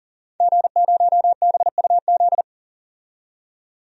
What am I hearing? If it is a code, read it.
Morse code, heard as a keyed tone, about 40 wpm, 700 Hz. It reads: G0BUZ
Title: DXCC-100 Call Signs - 5 Characters - 40wpm